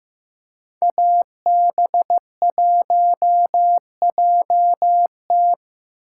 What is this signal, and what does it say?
Morse code, heard as a keyed tone, about 18 wpm, 700 Hz. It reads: AB1JT
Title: Speed Racing - USA 2x2 Call Signs 15wpm